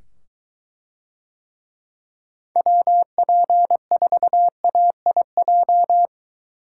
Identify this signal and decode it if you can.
Morse code, heard as a keyed tone, about 23 wpm, 700 Hz. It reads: WP4AIJ